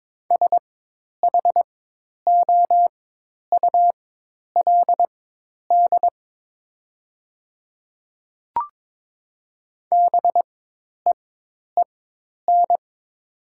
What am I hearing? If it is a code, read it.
Morse code, heard as a keyed tone, about 22 wpm, spaced as for 10 wpm, 700 Hz. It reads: SHOULD BEEN
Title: 4x Character Spacing - Top 300 Words - 22wpm